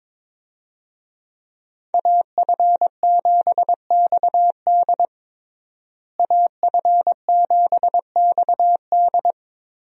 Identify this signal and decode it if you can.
Morse code, heard as a keyed tone, about 22 wpm, 700 Hz. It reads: AF7XD AF7XD
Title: USA Call Signs - 5 Characters - Repeated 2x - 22wpm